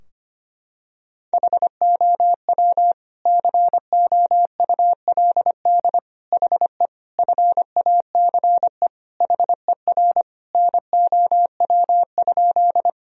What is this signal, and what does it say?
Morse code, heard as a keyed tone, about 25 wpm, 700 Hz. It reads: HOW COULD HE FACE HER NOW?
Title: Sentences from Top 1000 Words - Review Part 2 - 25wpm